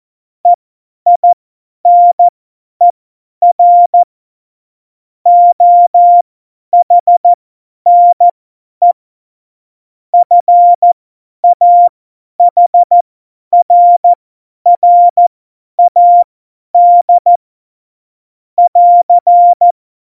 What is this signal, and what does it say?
Morse code, heard as a keyed tone, about 14 wpm, spaced as for 10 wpm, 700 Hz. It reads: EINER OHNE FAHRRAD <AR>
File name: Morsekurs / Witze / Witz 05. Februar 2019, 50 F70